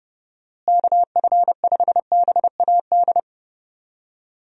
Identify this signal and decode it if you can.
Morse code, heard as a keyed tone, about 30 wpm, 700 Hz. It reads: KF5BAD
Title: USA Call Signs - 6 Characters - Repeated 2x - 30wpm